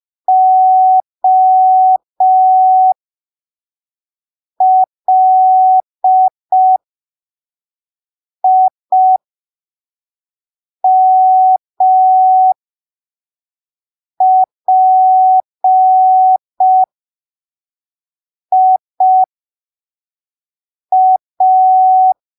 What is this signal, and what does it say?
Morse code, heard as a keyed tone, about 5 wpm, spaced as for 3 wpm, 750 Hz. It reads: OLIMPIA
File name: arquivos / texto / texto 08 05ppm